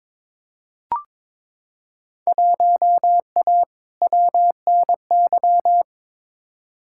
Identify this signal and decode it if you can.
Morse code, heard as a keyed tone, about 21 wpm, 700 Hz. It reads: 1A WNY
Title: Field Day Exchanges Encore - 22wpm